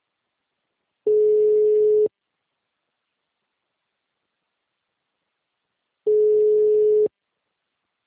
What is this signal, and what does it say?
Ringback tone (Europe, 425 Hz).